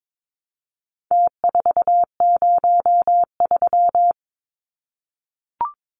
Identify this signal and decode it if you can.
Morse code, heard as a keyed tone, about 22 wpm, 700 Hz. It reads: T403
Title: Numbers - Contest Style - 4-Digit 22wpm